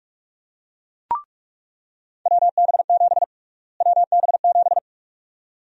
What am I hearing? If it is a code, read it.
Morse code, heard as a keyed tone, about 44 wpm, 700 Hz. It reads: WB7 WB7
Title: USA Call Sign Prefixes - Repeated 2x - 45wpm